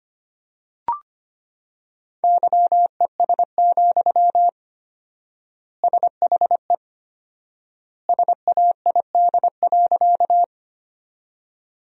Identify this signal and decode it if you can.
Morse code, heard as a keyed tone, about 25 wpm, 700 Hz. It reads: YES, SHE SAID.
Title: 4x Word-Spacing - Sentences from Top 500 Words - 25wpm